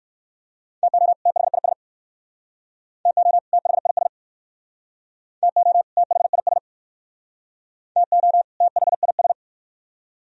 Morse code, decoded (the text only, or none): TO THIS TO THIS TO THIS TO THIS